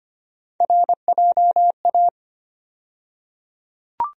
Morse code, heard as a keyed tone, about 25 wpm, 700 Hz. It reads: RJA
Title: USA Call Sign Postfixes - 3 Letters 25wpm